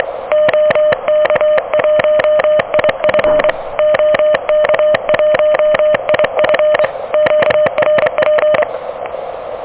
Morse code, heard as a keyed tone, about 25 wpm, 600 Hz. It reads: OK1IF OK1IF QRP
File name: OK QRP 2023 / OK QRP 2023